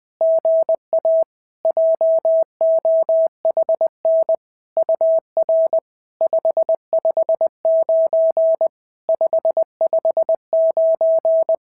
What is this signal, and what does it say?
Morse code, heard as a keyed tone, about 20 wpm, 650 Hz. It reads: GA JOHN UR 559 559
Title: SOTA QSO Practice 20wpm